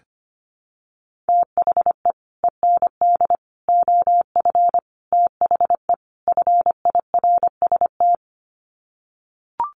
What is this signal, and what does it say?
Morse code, heard as a keyed tone, about 25 wpm, 700 Hz. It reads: THE END OF THE FIRST
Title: Sets of 5 Words Encore 25wpm